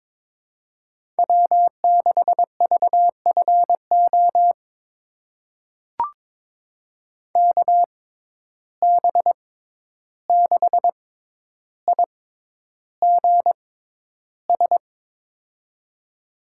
Morse code, heard as a keyed tone, about 21 wpm, spaced as for 11 wpm, 700 Hz. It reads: W6VFO KB6IGS
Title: USA Call Signs - 6x Character Spacing - 22wpm